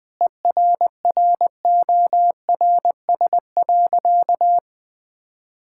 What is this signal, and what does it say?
Morse code, heard as a keyed tone, about 20 wpm, 700 Hz. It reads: ERRORS.
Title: Difficult Sentences with Hints - 20wpm